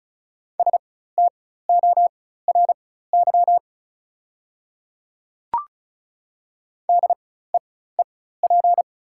Morse code, heard as a keyed tone, about 36 wpm, spaced as for 16 wpm, 690 Hz. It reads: STORY DEEP
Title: 4x Character Spacing - Top 500 Words - 35wpm